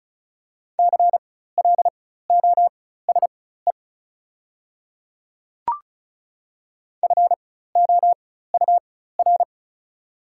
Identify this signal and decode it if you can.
Morse code, heard as a keyed tone, about 36 wpm, spaced as for 16 wpm, 690 Hz. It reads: CLOSE FOUR
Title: 4x Character Spacing - Top 300 Words - 35wpm